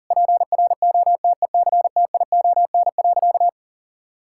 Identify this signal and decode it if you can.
Morse code, heard as a keyed tone, about 40 wpm, 700 Hz. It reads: PROTECTION.
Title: Sentences from Top 1600 Words 40wpm